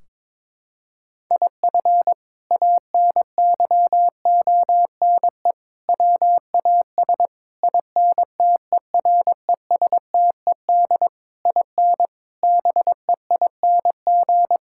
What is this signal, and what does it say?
Morse code, heard as a keyed tone, about 22 wpm, 700 Hz. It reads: IF ANYONE WAS INTERESTED IN BEING
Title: Sentences from Top 1600 Words 22wpm